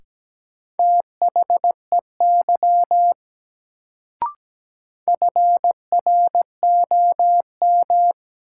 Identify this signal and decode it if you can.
Morse code, heard as a keyed tone, about 17 wpm, 700 Hz. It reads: THEY FROM
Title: Sets of 2 Words - New Word Drill - 17wpm